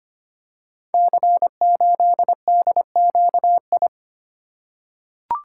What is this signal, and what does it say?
Morse code, heard as a keyed tone, about 25 wpm, 700 Hz. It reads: C8DQI
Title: Code Groups - 5 Letter:Numbers - 25wpm